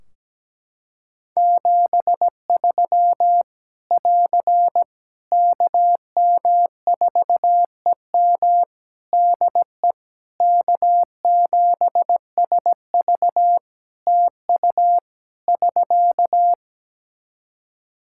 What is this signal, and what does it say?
Morse code, heard as a keyed tone, about 17 wpm, 700 Hz. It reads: 73 <AR> KM4EM DE K7SV TU <SK>
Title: Practice QSO - Exchange 4 - Ending - 17wpm